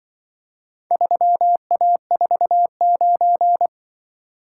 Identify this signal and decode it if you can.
Morse code, heard as a keyed tone, about 25 wpm, 700 Hz. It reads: 3A49